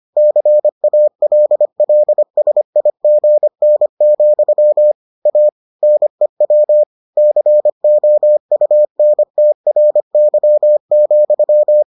Morse code, heard as a keyed tone, about 25 wpm, 600 Hz. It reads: CALLSIGN, A NEW COUNTRY,